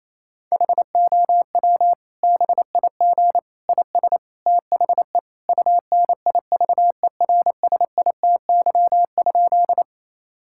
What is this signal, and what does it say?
Morse code, heard as a keyed tone, about 28 wpm, 700 Hz. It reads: HOW BIG IS THE UNIVERSITY?
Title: Sentences from Top 1000 Words - Review Part 5 - 28wpm